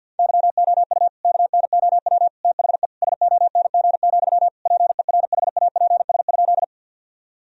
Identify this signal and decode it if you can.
Morse code, heard as a keyed tone, about 50 wpm, 700 Hz. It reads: YOU KNOW THE SONG, PERHAPS?
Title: Sentences from Top 1000 Words - Review Part 2 - 50wpm